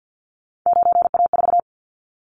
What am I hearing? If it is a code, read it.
Morse code, heard as a keyed tone, about 50 wpm, 700 Hz. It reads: 9A4
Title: Callsign Prefixes for Top 50 DXCC - 50wpm